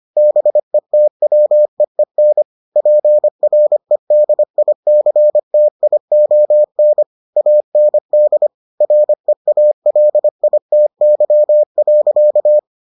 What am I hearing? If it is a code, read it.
Morse code, heard as a keyed tone, about 25 wpm, 600 Hz. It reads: BETWEEN PREDICTION AND REALITY.